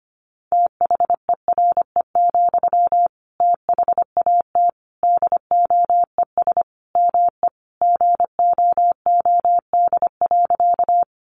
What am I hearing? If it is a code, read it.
Morse code, heard as a keyed tone, about 25 wpm, 700 Hz. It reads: THERE, THAT DOES ME GOOD.
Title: Sentences from Top 200 Words 25wpm